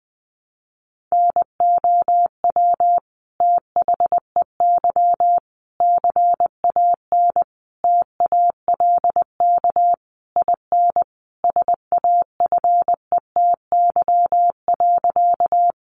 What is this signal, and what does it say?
Morse code, heard as a keyed tone, about 20 wpm, 700 Hz. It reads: NOW THEY CAN TALK IN SAFETY.
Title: Sentences from Top 700 Words 20wpm